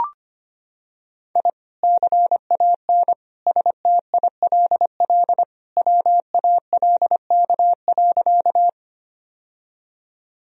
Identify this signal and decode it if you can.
Morse code, heard as a keyed tone, about 25 wpm, 700 Hz. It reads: I CAN STILL WALK.
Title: Sentences from Top 1000 Words - Review Part 1 - 25wpm